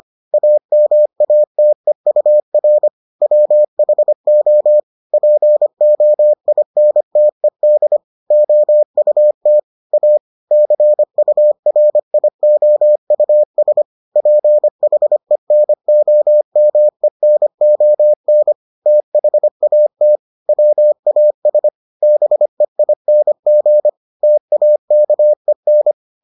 Morse code, heard as a keyed tone, about 25 wpm, 600 Hz. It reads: AMATEUR WHO POINTED OUT A CURIOUS PHENOMENON THAT WAS BEING TAKEN